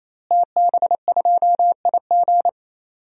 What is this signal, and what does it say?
Morse code, heard as a keyed tone, about 27 wpm, 700 Hz. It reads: TB2IG
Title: DXCC-100 Call Signs - 5 Characters - Repeated 3x - 28wpm